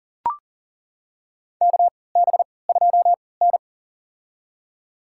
Morse code, heard as a keyed tone, about 40 wpm, spaced as for 23 wpm, 700 Hz. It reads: KB2N
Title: USA Call Signs - 4 Characters - 3x Character Spacing - 40wpm